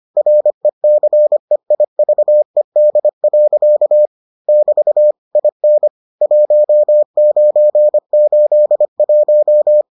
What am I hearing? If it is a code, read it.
Morse code, heard as a keyed tone, about 25 wpm, 600 Hz. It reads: RECEIVED. <BT> IN 1981